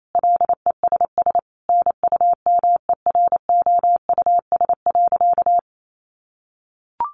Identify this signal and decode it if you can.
Morse code, heard as a keyed tone, about 28 wpm, 700 Hz. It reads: LESS NUMEROUS.
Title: Sentences from Top 1400 Words 28wpm